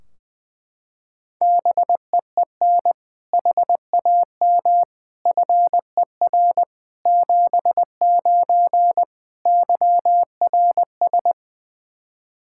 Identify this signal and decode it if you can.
Morse code, heard as a keyed tone, about 20 wpm, 700 Hz. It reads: BEEN HAM FER 79 YRS
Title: Practice QSO - Exchange 3 - More Info - 20wpm